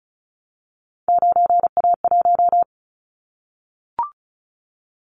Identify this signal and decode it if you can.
Morse code, heard as a keyed tone, about 36 wpm, 700 Hz. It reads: 9A1
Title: Callsign Prefixes for Top 25 DXCC - 35wpm